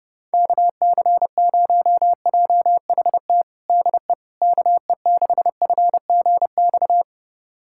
Morse code, heard as a keyed tone, about 30 wpm, 700 Hz. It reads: KC0JHT DE KE6FGX
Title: USA Callsign DE Callsign - 6 Characters - 30wpm